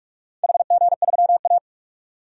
Morse code, heard as a keyed tone, about 44 wpm, 700 Hz. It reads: HG3A
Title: DXCC-100 Call Signs - 4 Characters - 45wpm